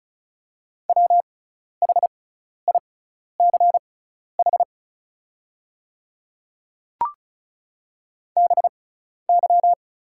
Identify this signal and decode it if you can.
Morse code, heard as a keyed tone, about 36 wpm, spaced as for 12 wpm, 700 Hz. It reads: WHICH BY
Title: 6x Character Spacing - Top 100 Words - 35wpm